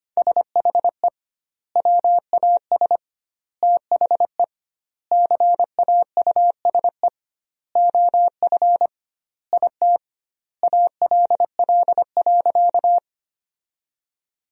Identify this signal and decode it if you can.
Morse code, heard as a keyed tone, about 25 wpm, 700 Hz. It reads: SHE WAS THE CAUSE OF IT ALL.
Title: 2x Word-Spacing - Sentences from Top 200 Words - 25wpm